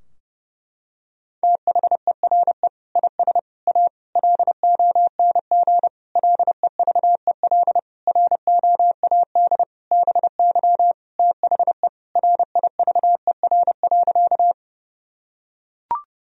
Morse code, heard as a keyed tone, about 30 wpm, 700 Hz. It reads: THERE IS A LONG LEVEL ROAD BY THE RIVER.
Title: Sentences from Top 1000 Words - Review Part 4 - 30wpm